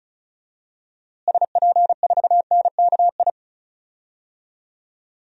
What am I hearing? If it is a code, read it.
Morse code, heard as a keyed tone, about 34 wpm, 700 Hz. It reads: SP4NKI
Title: DXCC-100 Call Signs - 6 Characters - Repeated 2x - 35wpm